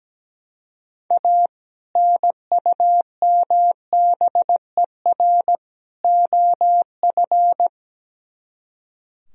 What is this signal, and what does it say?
Morse code, heard as a keyed tone, about 17 wpm, 700 Hz. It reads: A NUMBER OF
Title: Speed-Racing - Sets of 3 Words 17wpm